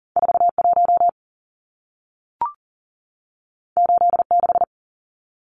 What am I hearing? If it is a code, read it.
Morse code, heard as a keyed tone, about 40 wpm, 700 Hz. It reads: 41 86
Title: Numbers - 2-Digit 40wpm